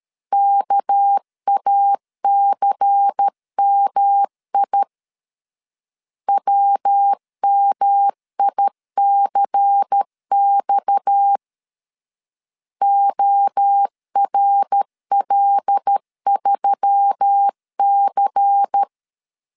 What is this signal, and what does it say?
Morse code, heard as a keyed tone, about 13 wpm, 800 Hz. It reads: KACMI WMICX ORL3C
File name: CW / DET / 1 6 1500C S B 12wpm 3431